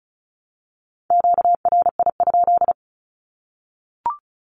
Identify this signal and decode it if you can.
Morse code, heard as a keyed tone, about 34 wpm, 700 Hz. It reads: QRI?
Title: Q-Codes Repeated 3x 35wpm